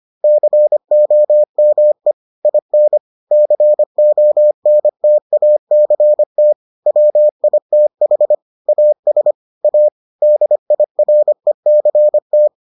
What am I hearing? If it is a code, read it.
Morse code, heard as a keyed tone, about 25 wpm, 600 Hz. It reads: COME IN CONTACT WITH AS A DIRECT